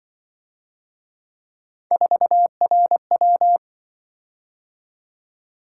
Morse code, heard as a keyed tone, about 25 wpm, 700 Hz. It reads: 4RW